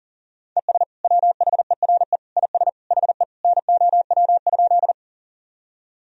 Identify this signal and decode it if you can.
Morse code, heard as a keyed tone, about 40 wpm, 700 Hz. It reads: ES WHERE IS HE NOW?
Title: Sentences from Top 200 Words and Abbreviations 40wpm